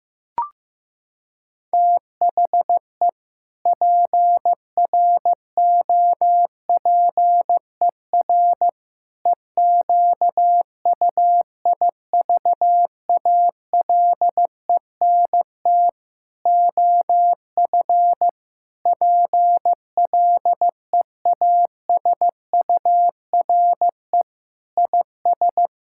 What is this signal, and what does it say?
Morse code, heard as a keyed tone, about 15 wpm, 700 Hz. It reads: THE PROPER EQUIVALENT OF PLEASURE IS
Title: Sentences from Top 1600 Words 15wpm